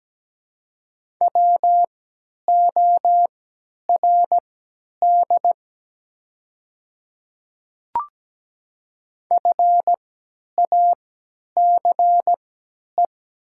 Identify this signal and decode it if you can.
Morse code, heard as a keyed tone, about 17 wpm, spaced as for 10 wpm, 700 Hz. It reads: WORD FACE
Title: 3x Character Spacing - Top 300 Words - 17wpm